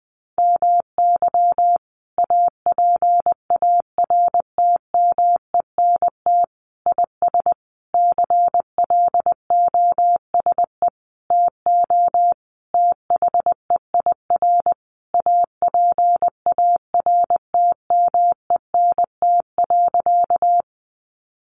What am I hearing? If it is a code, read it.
Morse code, heard as a keyed tone, about 19 wpm, 700 Hz. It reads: MY APARTMENT IS CLOSE TO THEIR APARTMENT.
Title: Sentences from Top 900 Words 20wpm